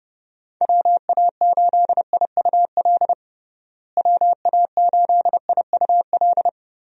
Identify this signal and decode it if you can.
Morse code, heard as a keyed tone, about 30 wpm, 700 Hz. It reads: WA8IUL WA8IUL
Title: USA Call Signs - 6 Characters - Repeated 3x - 30wpm